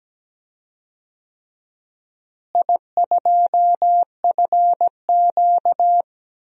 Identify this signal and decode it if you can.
Morse code, heard as a keyed tone, about 17 wpm, 700 Hz. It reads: I2FQ
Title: DXCC-100 Call Signs - 4 Characters - Repeated 3x - 17wpm